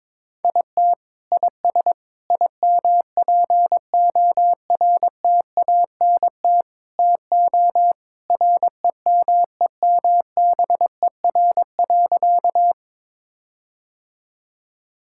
Morse code, heard as a keyed tone, about 22 wpm, 700 Hz. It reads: IT IS IMPORTANT TO REMEMBER.
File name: practice / 22wpm / Sentences with 4-Word Phrases - 22wpm